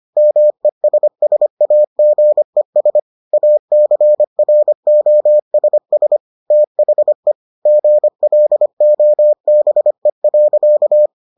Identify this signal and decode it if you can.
Morse code, heard as a keyed tone, about 25 wpm, 600 Hz. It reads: MESSAGES ACROSS THE GLOBE.